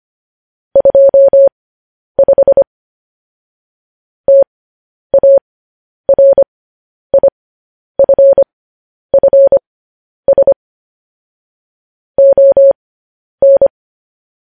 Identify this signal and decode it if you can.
Morse code, heard as a keyed tone, about 25 wpm, spaced as for 10 wpm, 550 Hz. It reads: 25 TARIFFS ON
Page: news-headlines-in-morse-code-at-10-wpm